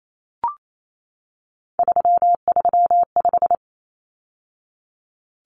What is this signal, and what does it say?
Morse code, heard as a keyed tone, about 28 wpm, 700 Hz. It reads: 335
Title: Numbers - 3-Digit 28wpm